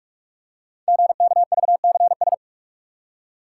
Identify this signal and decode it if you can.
Morse code, heard as a keyed tone, about 44 wpm, 700 Hz. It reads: GKVCS
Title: Code Groups - 5 Letters - 45wpm